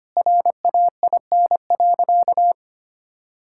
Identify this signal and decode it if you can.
Morse code, heard as a keyed tone, about 25 wpm, 700 Hz. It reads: RAIN.
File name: practice / 25wpm / Sentences with 3-Word Phrases - 25wpm